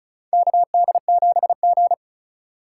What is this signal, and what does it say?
Morse code, heard as a keyed tone, about 36 wpm, 700 Hz. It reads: KD7G
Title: USA Call Signs - 3x Character Spacing - 35wpm